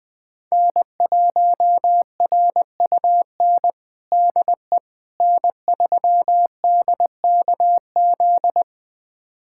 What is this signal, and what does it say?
Morse code, heard as a keyed tone, about 20 wpm, 700 Hz. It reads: N1RUN DE N3DKZ
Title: USA Callsign DE Callsign - 5 Characters - 20wpm